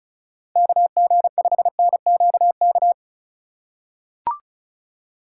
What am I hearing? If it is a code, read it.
Morse code, heard as a keyed tone, about 36 wpm, 700 Hz. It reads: KG5NQK
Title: USA Call Signs - 6 Characters - 3x Character Spacing - 35wpm